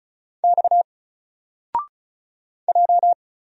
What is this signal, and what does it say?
Morse code, heard as a keyed tone, about 35 wpm, 700 Hz. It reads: X J
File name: Single-Letter-Number-35wpm